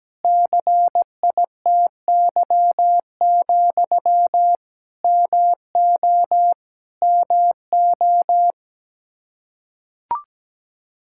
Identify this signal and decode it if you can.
Morse code, heard as a keyed tone, about 17 wpm, 700 Hz. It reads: CITY, MO MO
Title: Practice QSO - Exchange 1 - Beginning - 17wpm